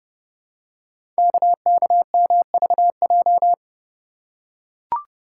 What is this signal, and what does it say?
Morse code, heard as a keyed tone, about 30 wpm, 700 Hz. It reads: KKMVJ
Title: Code Groups - 5 Letters - 30wpm